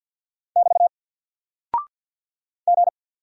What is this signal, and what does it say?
Morse code, heard as a keyed tone, about 50 wpm, 700 Hz. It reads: <BT> G
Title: Single Character (Letter:Number:Punctuation:Prosign) 50wpm